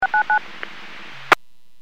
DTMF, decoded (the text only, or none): 699